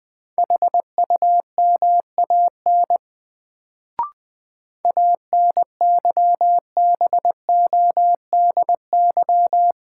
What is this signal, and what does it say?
Morse code, heard as a keyed tone, about 20 wpm, 700 Hz. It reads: HUMAN ANYBODY
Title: Sets of 5 Words - New Word Drill - 20wpm